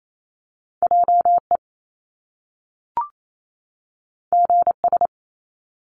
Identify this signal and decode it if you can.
Morse code, heard as a keyed tone, about 28 wpm, 700 Hz. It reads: JE GS
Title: Top Two Letter Combinations in Words 28wpm